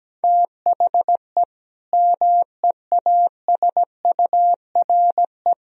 Morse code, heard as a keyed tone, about 17 wpm, 700 Hz. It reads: THE MEASURE
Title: Sentences from Top 1000 Words - Review Part 3 - 17wpm